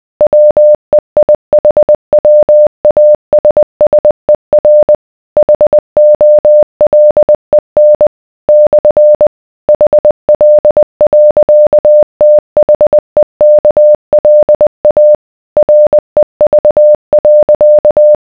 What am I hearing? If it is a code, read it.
Morse code, heard as a keyed tone, about 20 wpm, 600 Hz. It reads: WEIHWASSER HOLEN / HL.THEKLA REV.